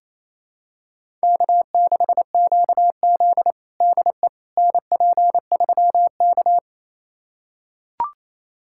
Morse code, heard as a keyed tone, about 28 wpm, 700 Hz. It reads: K6QZ DE NP3K
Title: USA Callsign DE Callsign - 4 Characters - 28wpm